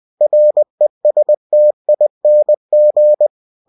Morse code, heard as a keyed tone, about 20 wpm, 600 Hz. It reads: RESTING
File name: cw-books / winnie-the-pooh / winnie-20wpm-page-0017